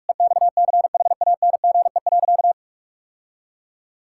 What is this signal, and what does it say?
Morse code, heard as a keyed tone, about 46 wpm, 700 Hz. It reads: EXCHANGE.
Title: Sentences from Top 1000 Words - Review Part 6 - 45wpm